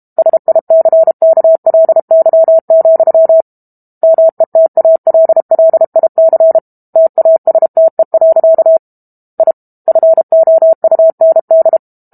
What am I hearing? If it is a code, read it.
Morse code, heard as a keyed tone, about 32 wpm, 650 Hz. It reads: SICKLY, METALLIC TASTE. I FOUND
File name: Book Two / 06-Chapter 6 - 32 WPM